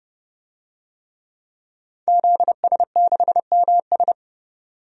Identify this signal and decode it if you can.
Morse code, heard as a keyed tone, about 31 wpm, 700 Hz. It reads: ZS6MS
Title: DXCC-100 Call Signs - 5 Characters - 30wpm